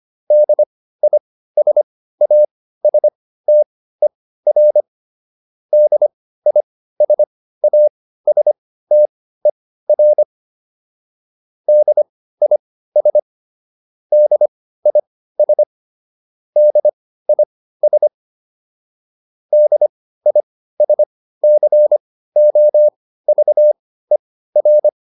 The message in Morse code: DISASTER DISASTER DIS DIS DIS DISCOVER

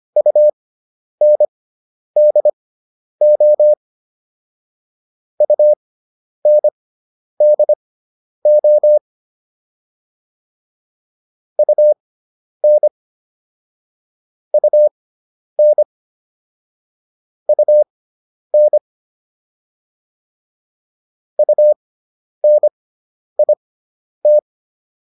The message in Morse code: UNDO UNDO UN UN UN UNIT

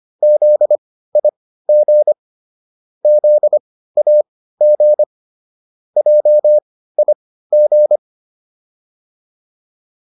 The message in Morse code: ZIG ZAG JIG

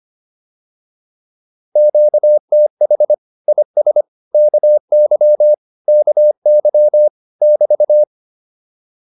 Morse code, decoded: QTH IS KY KY <BT>

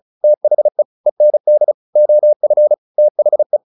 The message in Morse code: THE END OF THE

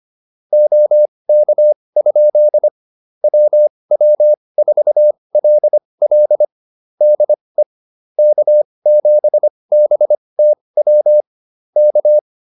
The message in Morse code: OK? WW4LL DE K7BTW K

